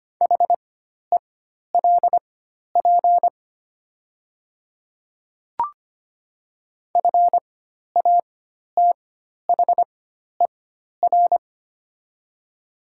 Morse code, heard as a keyed tone, about 25 wpm, spaced as for 12 wpm, 700 Hz. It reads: HELP FATHER
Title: 4x Character Spacing - Top 200 Words - 25wpm